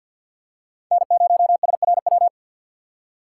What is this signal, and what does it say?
Morse code, heard as a keyed tone, about 52 wpm, 700 Hz. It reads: N0SRW
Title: USA Call Signs - 5 Characters - 4x Character Spacing - 50wpm